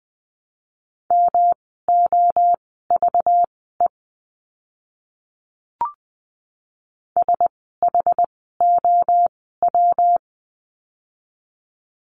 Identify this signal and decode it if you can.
Morse code, heard as a keyed tone, about 20 wpm, spaced as for 15 wpm, 700 Hz. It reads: MOVE SHOW
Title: 244 Lesson 39f - Letter z - Words 20wpm